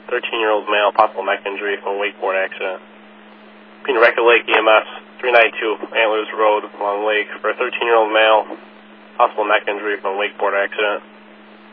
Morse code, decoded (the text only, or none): none